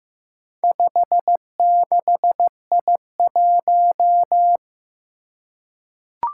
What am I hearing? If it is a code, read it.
Morse code, heard as a keyed tone, about 15 wpm, 700 Hz. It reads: 56I1
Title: Four Challenging Letter-Number Combinations 15wpm